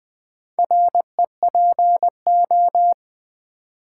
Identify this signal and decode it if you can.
Morse code, heard as a keyed tone, about 20 wpm, 700 Hz. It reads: REPO
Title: Build A Word - 2 Letters At A Time - 8-Letter Words - 20wpm